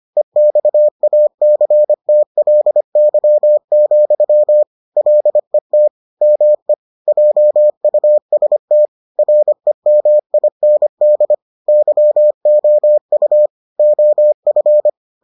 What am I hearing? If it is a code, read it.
Morse code, heard as a keyed tone, about 25 wpm, 600 Hz. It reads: EXACTLY, LET ME JUST REMIND YOU OF